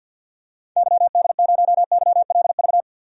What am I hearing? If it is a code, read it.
Morse code, heard as a keyed tone, about 50 wpm, 700 Hz. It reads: YD0YLV
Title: DXCC-100 Call Signs - 6 Characters - 50wpm